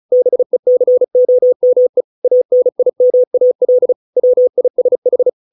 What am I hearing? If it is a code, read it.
Morse code, heard as a keyed tone, about 36 wpm, 500 Hz. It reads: BECOME ANIMAL WISH